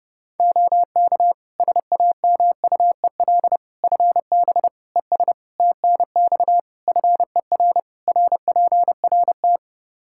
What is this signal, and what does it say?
Morse code, heard as a keyed tone, about 30 wpm, 700 Hz. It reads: OK SAMUEL FB ES TNX FER RPRT
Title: K1USN SST QSO Practice 30wpm